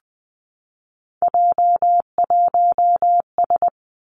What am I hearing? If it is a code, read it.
Morse code, heard as a keyed tone, about 20 wpm, 700 Hz. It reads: J1S
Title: Three Challenging Letter-Number Combinations 20wpm